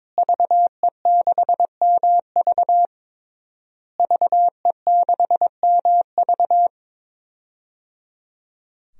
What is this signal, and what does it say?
Morse code, heard as a keyed tone, about 22 wpm, 700 Hz. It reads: VE6MV VE6MV
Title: DXCC-100 Call Signs - 5 Characters - Repeated 3x - 22wpm